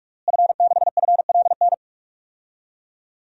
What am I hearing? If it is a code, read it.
Morse code, heard as a keyed tone, about 46 wpm, 700 Hz. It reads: F6FLN
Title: DXCC-100 Call Signs - 5 Characters - Repeated 3x - 45wpm